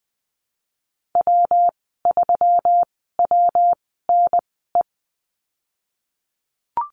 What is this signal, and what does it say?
Morse code, heard as a keyed tone, about 20 wpm, spaced as for 15 wpm, 700 Hz. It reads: W3WNE
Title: 128 Lesson 21g - Number 6 - Call Signs 20wpm